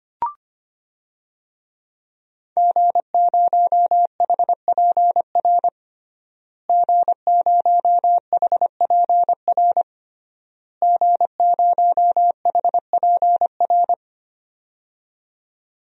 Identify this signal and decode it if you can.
Morse code, heard as a keyed tone, about 25 wpm, 700 Hz. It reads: G0HPR G0HPR G0HPR
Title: DXCC-100 Call Signs - 5 Characters - Repeated 3x - 25wpm